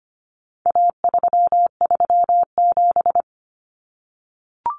Morse code, heard as a keyed tone, about 25 wpm, 700 Hz. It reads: A337
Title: Common Cut Numbers - 4-Digit 25wpm